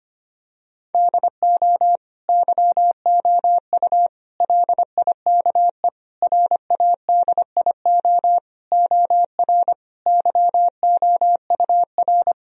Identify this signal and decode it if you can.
Morse code, heard as a keyed tone, about 25 wpm, 700 Hz. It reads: DO YOU LIKE RADIO OR YOUR
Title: Sentences from Top 1000 Words - Review Part 4 - 25wpm